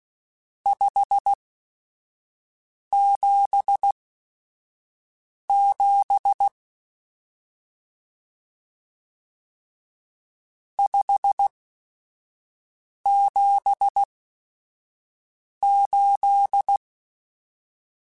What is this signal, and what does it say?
Morse code, heard as a keyed tone, about 16 wpm, spaced as for 5 wpm, 800 Hz. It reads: 577 578
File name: RSTs / RST-mp3s / Various RST @5